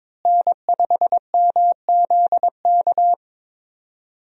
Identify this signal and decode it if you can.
Morse code, heard as a keyed tone, about 22 wpm, 700 Hz. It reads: N5MZK
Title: USA Call Signs - 5 Characters - Repeated 2x - 22wpm